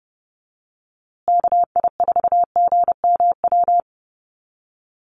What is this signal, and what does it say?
Morse code, heard as a keyed tone, about 30 wpm, 700 Hz. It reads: KI4GMW
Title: USA Call Signs Repeated 3x 30wpm